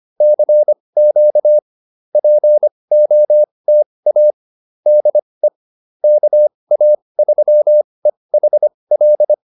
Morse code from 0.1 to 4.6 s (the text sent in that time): CQ POTA